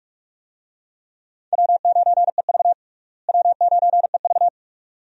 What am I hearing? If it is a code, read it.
Morse code, heard as a keyed tone, about 46 wpm, 700 Hz. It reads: W9EV W9EV
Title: USA Call Signs - 4 Characters - Repeated 2x - 45wpm